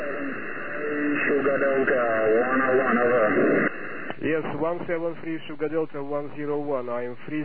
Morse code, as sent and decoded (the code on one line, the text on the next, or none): none